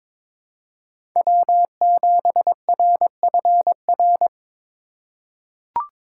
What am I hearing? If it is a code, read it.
Morse code, heard as a keyed tone, about 23 wpm, 700 Hz. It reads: W7RFR
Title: USA Call Signs - 5 Characters - 3x Character Spacing - 22wpm